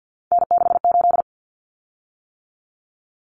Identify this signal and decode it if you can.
Morse code, heard as a keyed tone, about 50 wpm, 700 Hz. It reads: N68
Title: Common Cut Numbers - 3-Digit 50wpm